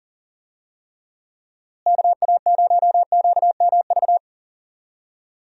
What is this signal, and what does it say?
Morse code, heard as a keyed tone, about 40 wpm, 700 Hz. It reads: KA0QMV
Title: USA Call Signs - 6 Characters - Repeated 3x - 40wpm